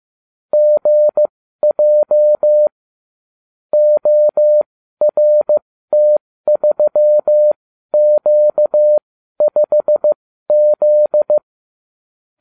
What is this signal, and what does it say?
Morse code, heard as a keyed tone, about 15 wpm, spaced as for 12 wpm, 600 Hz. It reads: GJ ORT3Q5Z